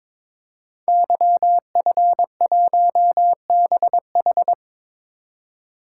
Morse code, heard as a keyed tone, about 22 wpm, 700 Hz. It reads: YF1BH
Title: Code Groups - 5 Letter:Numbers - 22wpm